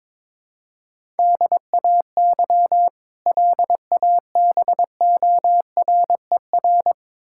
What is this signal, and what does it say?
Morse code, heard as a keyed tone, about 22 wpm, 700 Hz. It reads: DAY LABORER
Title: K1USN SST QSO Practice 22wpm